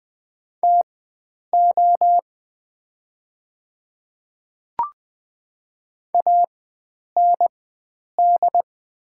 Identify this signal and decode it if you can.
Morse code, heard as a keyed tone, about 20 wpm, spaced as for 9 wpm, 690 Hz. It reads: TO AND
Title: 4x Character Spacing - Top 100 Words - 20wpm